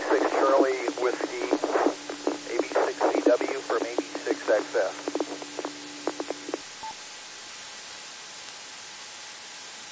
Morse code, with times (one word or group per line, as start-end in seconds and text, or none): none